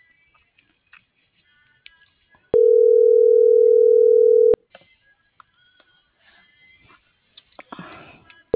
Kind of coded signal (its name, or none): telephone call progress tone